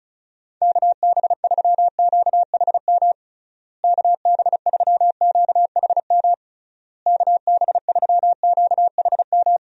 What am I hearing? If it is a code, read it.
Morse code, heard as a keyed tone, about 34 wpm, 700 Hz. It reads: KB3QHM KB3QHM KB3QHM
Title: USA Call Signs - 6 Characters - Repeated 3x - 35wpm